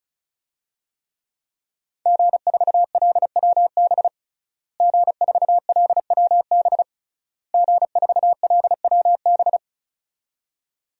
Morse code, heard as a keyed tone, about 34 wpm, 700 Hz. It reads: G4LWB G4LWB G4LWB
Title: DXCC-100 Call Signs - 5 Characters - Repeated 3x - 35wpm